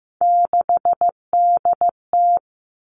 Morse code, heard as a keyed tone, about 15 wpm, 700 Hz. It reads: 6DT
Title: Three Challenging Letter-Number Combinations 15wpm